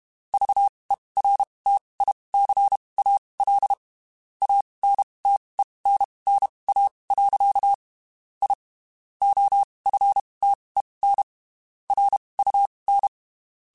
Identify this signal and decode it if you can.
Morse code, heard as a keyed tone, about 32 wpm, spaced as for 23 wpm, 800 Hz. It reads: VERTICAL ANTENNA. I OFTEN RUN
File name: (spaced) / QSOs Spaced-mp3s / WA7QOX 2 @30 sp